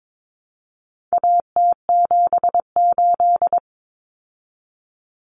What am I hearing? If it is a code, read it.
Morse code, heard as a keyed tone, about 22 wpm, 700 Hz. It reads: AT78
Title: Common Cut Numbers - 4-Digit 22wpm